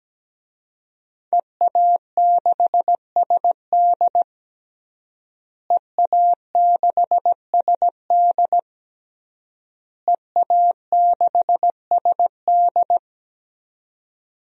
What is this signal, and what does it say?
Morse code, heard as a keyed tone, about 17 wpm, 700 Hz. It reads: EA6SD EA6SD EA6SD
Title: DXCC-100 Call Signs - 5 Characters - Repeated 3x - 17wpm